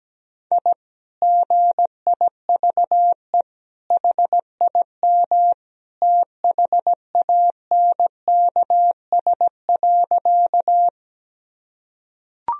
Morse code, heard as a keyed tone, about 17 wpm, 700 Hz. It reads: I GIVE HIM THANKS.
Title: Sentences from Top 1000 Words - Review Part 3 - 17wpm